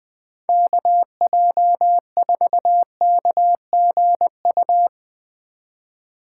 Morse code, heard as a keyed tone, about 21 wpm, 700 Hz. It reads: KJ4KGU